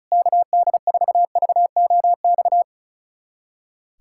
Morse code, heard as a keyed tone, about 36 wpm, 700 Hz. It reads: KD4VOX
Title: USA Callsign DE Callsign - 6 Characters - 35wpm